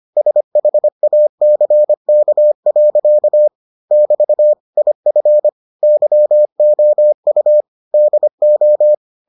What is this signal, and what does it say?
Morse code, heard as a keyed tone, about 25 wpm, 600 Hz. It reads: SHACK. <BT> IF YOU DO